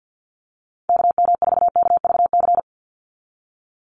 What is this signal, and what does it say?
Morse code, heard as a keyed tone, about 50 wpm, 700 Hz. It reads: KM4KVC
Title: USA Call Signs 50wpm